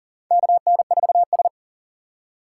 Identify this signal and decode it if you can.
Morse code, heard as a keyed tone, about 40 wpm, 700 Hz. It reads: KN4S
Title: USA Call Signs - 4 Characters - 4x Character Spacing - 40wpm